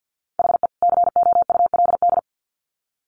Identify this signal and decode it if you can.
Morse code, heard as a keyed tone, about 50 wpm, 700 Hz. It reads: HE COULD
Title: Sets of 2 Words 50wpm